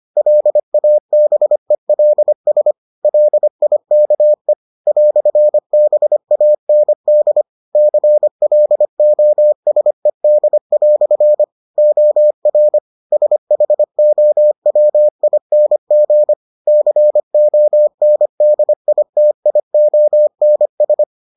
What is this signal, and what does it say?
Morse code, heard as a keyed tone, about 25 wpm, 600 Hz. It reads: LABELS LIKE "BAND CLOSED" OR SHOWING CONDITIONS